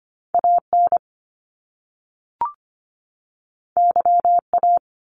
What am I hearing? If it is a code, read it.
Morse code, heard as a keyed tone, about 25 wpm, 700 Hz. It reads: AN YA
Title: Top Two Letter Combinations in Words 25wpm